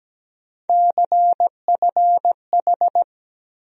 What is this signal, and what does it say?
Morse code, heard as a keyed tone, about 17 wpm, 700 Hz. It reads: CFH
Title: USA Call Sign Postfixes - 3 Letters 17wpm